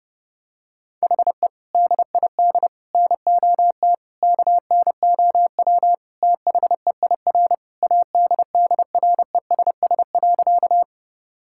Morse code, heard as a keyed tone, about 30 wpm, 700 Hz. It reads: HE DID NOT KNOW THEIR ADDRESS.
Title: Sentences from Top 1000 Words - Review Part 6 - 30wpm